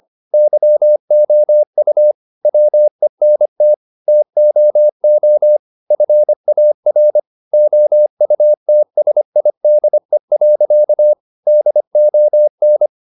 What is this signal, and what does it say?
Morse code, heard as a keyed tone, about 25 wpm, 600 Hz. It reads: YOU WENT TOO FAR OUTSIDE. DON